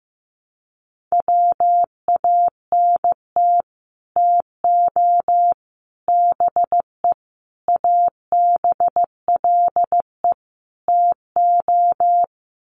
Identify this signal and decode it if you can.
Morse code, heard as a keyed tone, about 15 wpm, 700 Hz. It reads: WANT TO BE ABLE TO
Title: Sets of 5 Words 15wpm